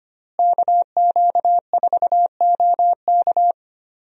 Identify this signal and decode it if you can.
Morse code, heard as a keyed tone, about 25 wpm, 700 Hz. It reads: KQ4OK
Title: USA Call Signs - 5 Characters - Repeated 3x - 25wpm